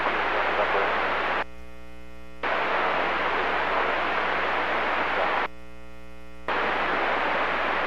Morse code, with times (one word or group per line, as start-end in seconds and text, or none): none